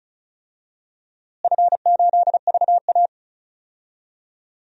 Morse code, heard as a keyed tone, about 36 wpm, 700 Hz. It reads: F8VA
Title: DXCC-100 Call Signs - 4 Characters - Repeated 2x - 35wpm